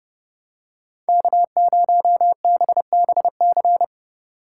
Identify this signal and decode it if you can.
Morse code, heard as a keyed tone, about 30 wpm, 700 Hz. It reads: K0BBC
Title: USA Call Signs - 5 Characters - 4x Character Spacing - 30wpm